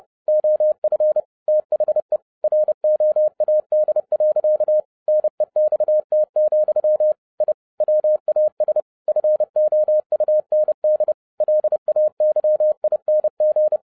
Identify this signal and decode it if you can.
Morse code, heard as a keyed tone, about 30 wpm, 610 Hz. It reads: OF THE ROAD. NEXT, I WAS FOUND LAYING
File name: 2011-07-mp3 / n55 BACK FROM A BIG ACCIDENT 30wpm